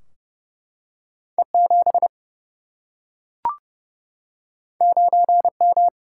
Morse code, heard as a keyed tone, about 30 wpm, 700 Hz. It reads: E7 9M
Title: Single-Letter Single-Number 30wpm